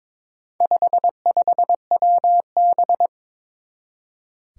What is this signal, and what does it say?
Morse code, heard as a keyed tone, about 22 wpm, 700 Hz. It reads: 55WB
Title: Four Challenging Letter-Number Combinations 22wpm